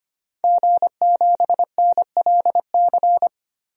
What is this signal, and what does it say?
Morse code, heard as a keyed tone, about 25 wpm, 700 Hz. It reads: G7NLC
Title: DXCC-100 Call Signs - 5 Characters - Repeated 2x - 25wpm